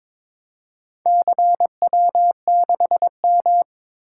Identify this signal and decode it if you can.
Morse code, heard as a keyed tone, about 22 wpm, 700 Hz. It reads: CW6M
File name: practice / 22wpm / Four Letter-Number Combinations - 22wpm